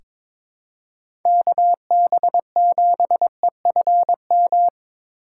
Morse code, heard as a keyed tone, about 22 wpm, 700 Hz. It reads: KB7EFM
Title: USA Call Signs - 6 Characters - 4x Character Spacing - 22wpm